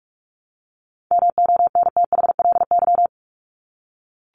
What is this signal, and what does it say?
Morse code, heard as a keyed tone, about 44 wpm, 700 Hz. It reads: MONTHLY
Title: LY Suffix - Top 100 Words with Base Words 45wpm